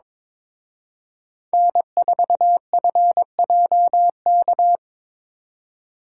Morse code, heard as a keyed tone, about 26 wpm, 700 Hz. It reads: N4FJK
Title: Speed Racing - USA 1x3 Call Signs 22wpm